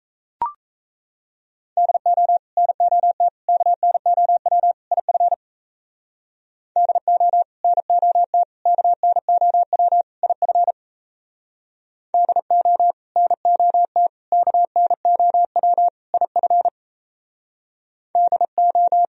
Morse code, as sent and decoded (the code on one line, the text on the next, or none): -.. --- / -. --- - / -.- -. --- .-- / .. ..-. / -.. --- / -. --- - / -.- -. --- .-- / .. ..-. / -.. --- / -. --- - / -.- -. --- .-- / .. ..-. / -.. ---
DO NOT KNOW IF DO NOT KNOW IF DO NOT KNOW IF DO